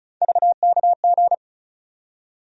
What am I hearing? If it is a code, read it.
Morse code, heard as a keyed tone, about 34 wpm, 700 Hz. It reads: VKG
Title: USA Call Sign Postfixes - 3 Letters 35wpm